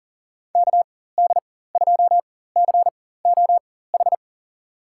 Morse code, heard as a keyed tone, about 38 wpm, spaced as for 18 wpm, 700 Hz. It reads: KD2COH
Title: USA Call Signs - 4x Character Spacing - 40wpm